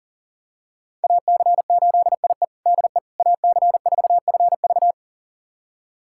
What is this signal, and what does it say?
Morse code, heard as a keyed tone, about 40 wpm, 690 Hz. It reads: AC8IE DE AC4FV
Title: USA Callsign DE Callsign - 5 Characters - 40wpm